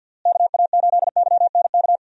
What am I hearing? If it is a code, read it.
Morse code, heard as a keyed tone, about 47 wpm, 700 Hz. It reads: KA8YNK
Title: USA Call Signs - 6 Characters - Repeated 3x - 50wpm